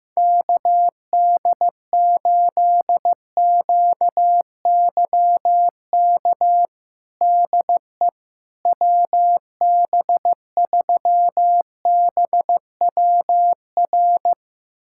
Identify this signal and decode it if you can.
Morse code, heard as a keyed tone, about 16 wpm, 700 Hz. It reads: KD8QYK DE WB3BWR
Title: USA Callsign DE Callsign 15wpm